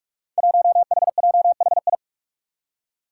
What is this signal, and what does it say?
Morse code, heard as a keyed tone, about 46 wpm, 700 Hz. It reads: 1HJHI
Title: Five Challenging Letter-Number Combinations 45wpm